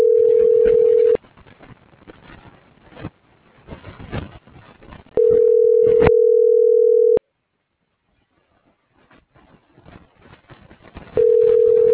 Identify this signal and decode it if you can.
Ringback tone (North America), 440 + 480 Hz.